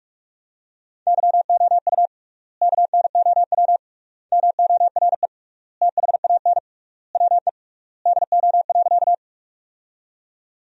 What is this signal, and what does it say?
Morse code, heard as a keyed tone, about 46 wpm, 700 Hz. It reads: YOU KNOW MORE THAN WE DO.
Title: 3x Word-Spacing - Sentences from Top 100 Words - 45wpm